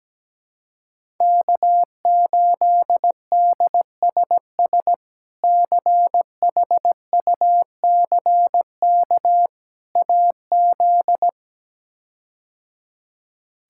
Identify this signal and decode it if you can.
Morse code, heard as a keyed tone, about 17 wpm, 700 Hz. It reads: K8DSS CHUCK AZ
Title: K1USN SST QSO Practice - 17wpm